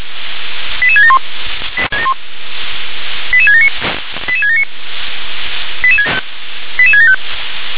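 Five-tone selective call (ZVEI), heard as quick runs of five tones, tones of about 70 ms. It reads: 88671 88679 88679 88675
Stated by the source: ZVEI five-tone selective call received on an SDR.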